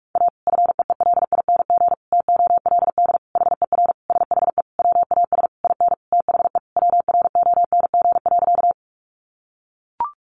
Morse code, heard as a keyed tone, about 44 wpm, 700 Hz. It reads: A FEELING TOLD HER SHE WAS IN THE WRONG.
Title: Sentences from Top 1200 Words 45wpm